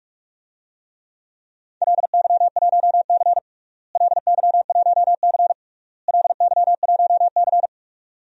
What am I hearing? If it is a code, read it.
Morse code, heard as a keyed tone, about 43 wpm, 700 Hz. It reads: LY1C LY1C LY1C
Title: DXCC-100 Call Signs - 4 Characters - Repeated 3x - 45wpm